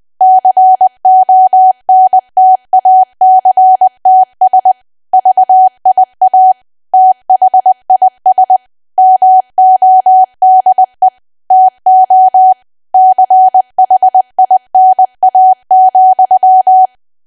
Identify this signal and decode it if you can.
Morse code, heard as a keyed tone, about 20 wpm, 750 Hz. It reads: CONTACTS VIA THIS MODE TO CHINA,
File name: cwa / W1AW-Level3 / 20WPM